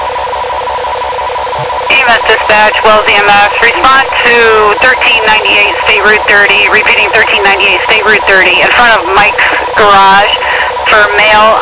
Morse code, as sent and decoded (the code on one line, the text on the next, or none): none